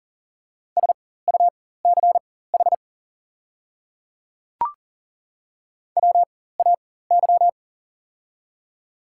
Morse code, heard as a keyed tone, about 40 wpm, spaced as for 19 wpm, 700 Hz. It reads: SUCH WAY
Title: 4x Character Spacing - Top 300 Words - 40wpm